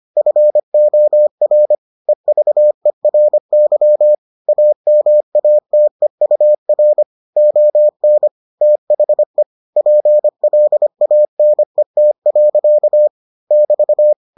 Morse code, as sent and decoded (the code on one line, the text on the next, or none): ..-. --- .-. / . ...- . .-. -.-- / .- -- .- - . ..- .-. / --- -. / - .... / .--. .-.. .- -. - .-.-.- / -...-
FOR EVERY AMATEUR ON THE PLANET. <BT>